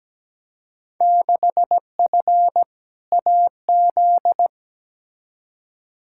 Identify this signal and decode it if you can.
Morse code, heard as a keyed tone, about 17 wpm, 700 Hz. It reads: 6F AZ
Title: Field Day Exchanges Encore - 17wpm